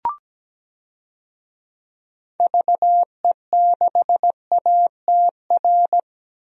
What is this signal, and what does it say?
Morse code, heard as a keyed tone, about 17 wpm, 700 Hz. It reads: VE6ATR
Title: DXCC-100 Call Signs - 6 Characters - 17wpm